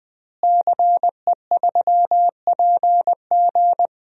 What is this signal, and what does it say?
Morse code, heard as a keyed tone, about 20 wpm, 700 Hz. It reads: CE3PG
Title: DXCC-100 Call Signs - 5 Characters - Repeated 2x - 20wpm